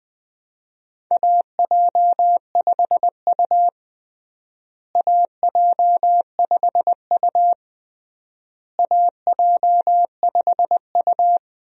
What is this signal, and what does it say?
Morse code, heard as a keyed tone, about 20 wpm, 700 Hz. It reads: AJ5U AJ5U AJ5U
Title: USA Call Signs - 4 Characters - Repeated 3x - 20wpm